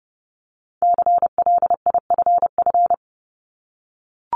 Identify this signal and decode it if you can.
Morse code, heard as a keyed tone, about 30 wpm, 700 Hz. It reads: CLIFF
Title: Multi-letter Phonemes Repeated 3x 30wpm